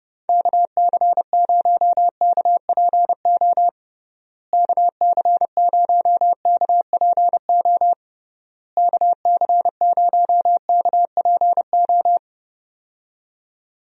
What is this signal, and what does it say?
Morse code, heard as a keyed tone, about 32 wpm, 700 Hz. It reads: KC0KPO KC0KPO KC0KPO
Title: USA Call Signs - 6 Characters - Repeated 3x - 30wpm